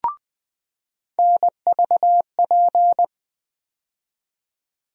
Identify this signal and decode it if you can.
Morse code, heard as a keyed tone, about 20 wpm, 700 Hz. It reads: NVP
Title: USA Call Sign Postfixes - 3 Letters 20wpm